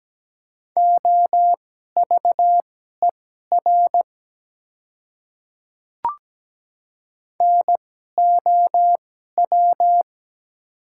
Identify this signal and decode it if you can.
Morse code, heard as a keyed tone, about 17 wpm, spaced as for 12 wpm, 700 Hz. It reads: OVER NOW
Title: 2x Character Spacing - Top 100 Words - 17wpm